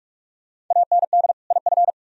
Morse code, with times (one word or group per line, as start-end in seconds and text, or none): 0.7-1.3 AND
1.5-1.9 IF